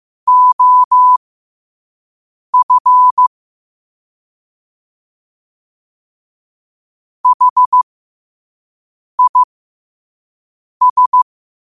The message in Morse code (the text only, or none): OF HIS